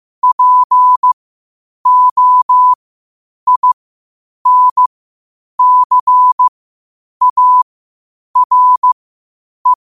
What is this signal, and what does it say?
Morse code, heard as a keyed tone, about 15 wpm, spaced as for 8 wpm, 1000 Hz. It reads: POINCARE